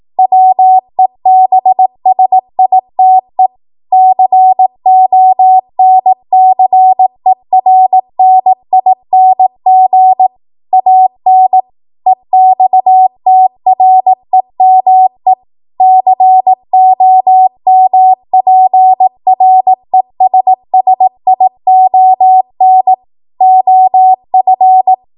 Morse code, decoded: WEBSITE CONCERNING AN EXTREME COMPRESSION OF